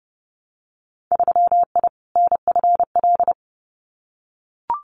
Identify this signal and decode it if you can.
Morse code, heard as a keyed tone, about 30 wpm, 700 Hz. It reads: 3I NFL
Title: Winter Field Day Exchanges 30wpm